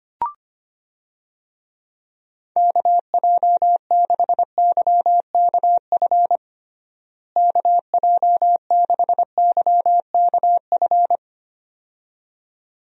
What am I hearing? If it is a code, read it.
Morse code, heard as a keyed tone, about 25 wpm, 700 Hz. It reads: KJ6YKF KJ6YKF
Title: USA Call Signs - 6 Characters - Repeated 2x - 25wpm